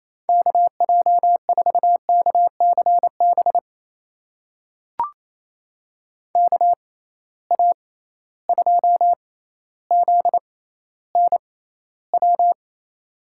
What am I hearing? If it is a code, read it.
Morse code, heard as a keyed tone, about 28 wpm, spaced as for 14 wpm, 700 Hz. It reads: KJ4KCB KA2ZNW
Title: USA Call Signs - 6 Characters - 6x Character Spacing - 28wpm